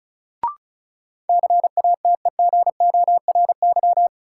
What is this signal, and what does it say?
Morse code, heard as a keyed tone, about 34 wpm, 700 Hz. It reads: CATEGORY
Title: Sentences from Top 1000 Words - Review Part 7 - New Word Drill - 35wpm